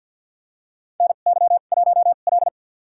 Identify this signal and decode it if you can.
Morse code, heard as a keyed tone, about 52 wpm, spaced as for 37 wpm, 700 Hz. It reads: NY1L
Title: USA Call Signs - 4 Characters - 2x Character Spacing - 50wpm